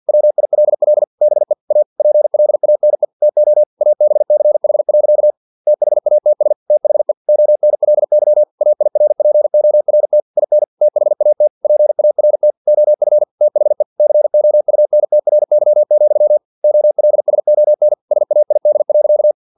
Morse code, heard as a keyed tone, about 50 wpm, 600 Hz. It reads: WILL BE A PLANE TO ABCH. THATS THE ONLY AIRPORT IN THAT PART OF THE COUNTRY, OLSON SAID.
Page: Never 49wpm